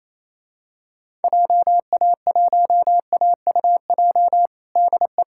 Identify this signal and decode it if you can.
Morse code, heard as a keyed tone, about 27 wpm, 700 Hz. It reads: JA1AUJ DE
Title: USA Callsign DE Callsign 28wpm